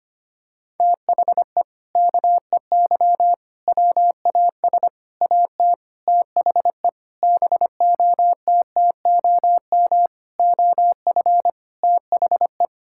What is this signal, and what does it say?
Morse code, heard as a keyed tone, about 25 wpm, 700 Hz. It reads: THE KEY WAS AT THE BOTTOM OF THE
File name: practice / 25wpm / Sentences with 4-Word Phrases - 25wpm